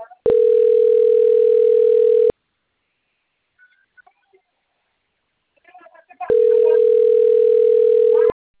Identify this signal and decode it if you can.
Ringback tone (North America), 440 + 480 Hz.